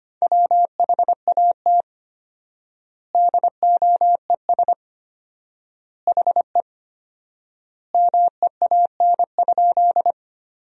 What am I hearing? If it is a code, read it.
Morse code, heard as a keyed tone, about 25 wpm, 700 Hz. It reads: WHAT DOES HE MEAN?
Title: 4x Word-Spacing - Sentences from Top 200 Words - 25wpm